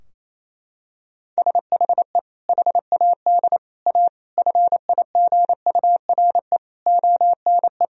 Morse code, heard as a keyed tone, about 28 wpm, 700 Hz. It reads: SHE HAD A FIGURE ONE
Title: Sentences from Top 1000 Words - Review Part 2 - 28wpm